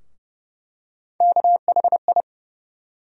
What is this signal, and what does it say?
Morse code, heard as a keyed tone, about 30 wpm, 700 Hz. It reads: KHI
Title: USA Call Sign Postfixes - 3 Letters 30wpm